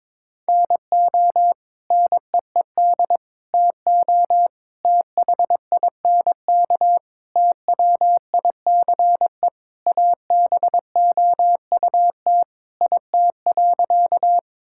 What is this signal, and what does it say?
Morse code, heard as a keyed tone, about 22 wpm, 700 Hz. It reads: NO NEED TO THINK TWICE ABOUT IT.
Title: Sentences from Top 1600 Words 22wpm